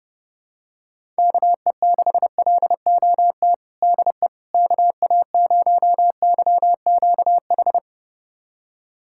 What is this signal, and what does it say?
Morse code, heard as a keyed tone, about 30 wpm, 700 Hz. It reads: KE6LOT DE KA0YQH
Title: USA Callsign DE Callsign 30wpm